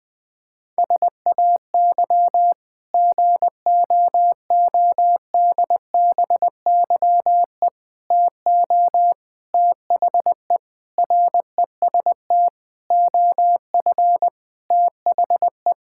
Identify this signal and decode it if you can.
Morse code, heard as a keyed tone, about 20 wpm, 700 Hz. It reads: SAY GOODBYE TO THE REST OF THE
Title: Sentences with 5-Word Phrases - 20wpm